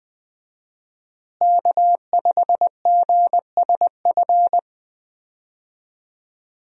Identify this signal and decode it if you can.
Morse code, heard as a keyed tone, about 20 wpm, 700 Hz. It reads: K5GSF
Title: USA Call Signs - 5 Characters - Repeated 3x - 20wpm